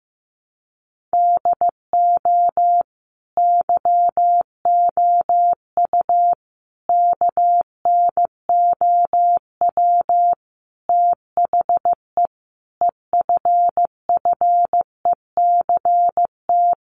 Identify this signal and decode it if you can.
Morse code, heard as a keyed tone, about 15 wpm, 700 Hz. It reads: DO YOU KNOW THE EFFECT
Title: Sentences from Top 1400 Words 15wpm